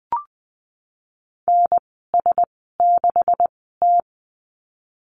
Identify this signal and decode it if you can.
Morse code, heard as a keyed tone, about 20 wpm, spaced as for 15 wpm, 700 Hz. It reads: NS6T
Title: 147 Lesson 24g - Letter y - Call Signs 20wpm